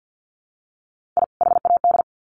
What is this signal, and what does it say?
Morse code, heard as a keyed tone, about 50 wpm, 700 Hz. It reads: I HAD
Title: Sets of 2 Words 50wpm